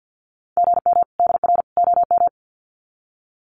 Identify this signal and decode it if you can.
Morse code, heard as a keyed tone, about 50 wpm, 700 Hz. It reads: GM DR OM
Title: Common QSO Phrases 50wpm